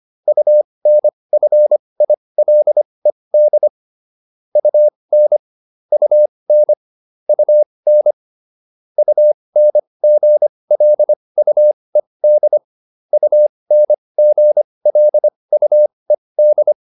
Morse code, heard as a keyed tone, about 25 wpm, spaced as for 20 wpm, 600 Hz. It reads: UNFILED UN UN UN UNGLUED UNGLUED